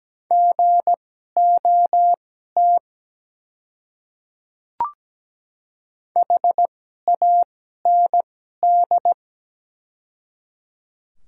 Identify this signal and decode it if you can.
Morse code, heard as a keyed tone, about 17 wpm, spaced as for 12 wpm, 700 Hz. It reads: GOT HAND
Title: 2x Character Spacing - Top 300 Words - 17wpm